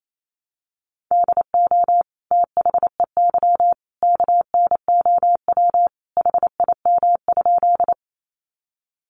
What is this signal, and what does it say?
Morse code, heard as a keyed tone, about 28 wpm, 700 Hz. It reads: DO THEY KNOW HIM?
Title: Sentences from Top 100 Words 28wpm